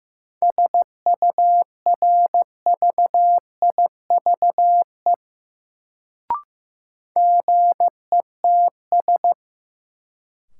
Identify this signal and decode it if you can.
Morse code, heard as a keyed tone, about 15 wpm, 700 Hz. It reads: SURVIVE GETS
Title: Sentences from Top 1300 Words - New Word Drill Alt - 15wpm